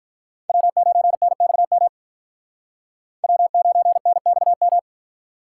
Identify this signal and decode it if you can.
Morse code, heard as a keyed tone, about 53 wpm, 700 Hz. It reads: W9NXM W9NXM